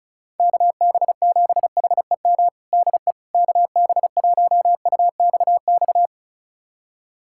Morse code, heard as a keyed tone, about 35 wpm, 700 Hz. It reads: KB7HEM DE KB1UXX
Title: USA Callsign DE Callsign - 6 Characters - 35wpm